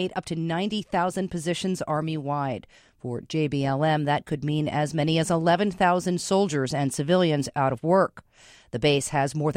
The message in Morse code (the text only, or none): none